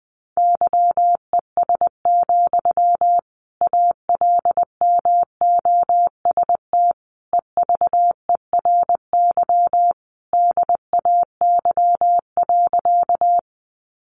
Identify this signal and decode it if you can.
Morse code, heard as a keyed tone, about 20 wpm, 700 Hz. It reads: YES, ALMOST EVERY DAY.
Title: Sentences from Top 700 Words 20wpm